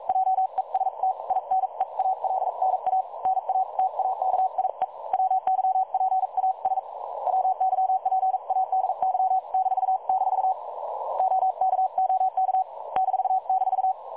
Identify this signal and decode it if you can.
Morse code, heard as a keyed tone, about 42 wpm, 760 Hz. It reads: JEST INEALNA NA TAKIE WYPAS WYPADY -- OK OK --
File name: dlazaawansowanych / nagraniazpasma / websdr recording 3565.0kHz